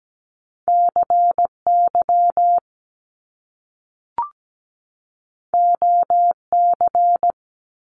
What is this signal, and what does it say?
Morse code, heard as a keyed tone, about 17 wpm, 700 Hz. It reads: CY OC
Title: Top Two Letter Combinations in Words 17wpm